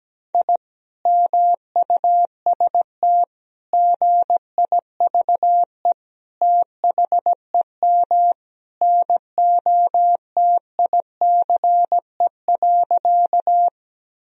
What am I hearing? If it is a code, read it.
Morse code, heard as a keyed tone, about 17 wpm, 700 Hz. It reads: I MUST GIVE THEM NOTICE.
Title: Sentences from Top 1000 Words - Review Part 2 - 17wpm